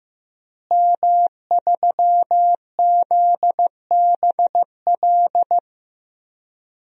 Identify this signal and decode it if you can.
Morse code, heard as a keyed tone, about 15 wpm, 700 Hz. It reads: M3ZBL
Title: DXCC-100 Call Signs - 5 Characters - Repeated 2x - 15wpm